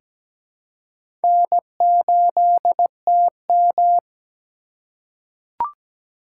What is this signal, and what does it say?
Morse code, heard as a keyed tone, about 17 wpm, 700 Hz. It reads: N8TM
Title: USA Call Signs - 4 Characters - 4x Character Spacing - 17wpm